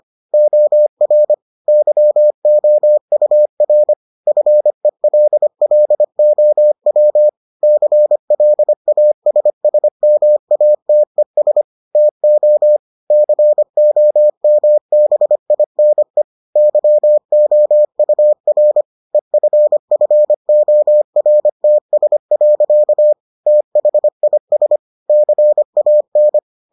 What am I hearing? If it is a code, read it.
Morse code, heard as a keyed tone, about 25 wpm, 600 Hz. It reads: OR YOUR FELLOW CLASSMATES TO COMBINE YOUR EFFORTS. THIS CAN